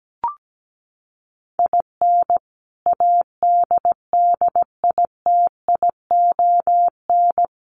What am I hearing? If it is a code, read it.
Morse code, heard as a keyed tone, about 17 wpm, 700 Hz. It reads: IN ADDITION
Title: Sets of 4 Words 17wpm